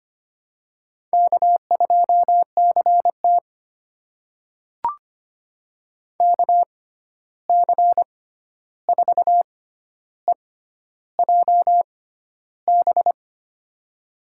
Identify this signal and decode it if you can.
Morse code, heard as a keyed tone, about 25 wpm, spaced as for 11 wpm, 700 Hz. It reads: K2CT KC4EJB
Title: USA Call Signs - 6x Character Spacing - 25wpm